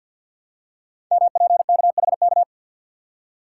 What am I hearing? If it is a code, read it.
Morse code, heard as a keyed tone, about 48 wpm, 700 Hz. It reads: MPKHK
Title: Code Groups - 5 Letter:Numbers - 50wpm